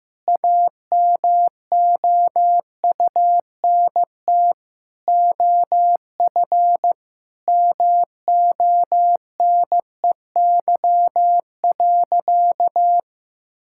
Morse code, heard as a keyed tone, about 15 wpm, 700 Hz. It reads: AMOUNT OF MONEY.
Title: Sentences from Top 1000 Words - Review Part 6 - 15wpm